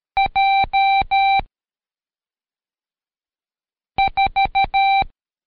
Morse code, heard as a keyed tone, about 13 wpm, 780 Hz. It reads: J 4
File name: Code Quick / AudioDownld2 / Seg104-Day12-FiinalExam 1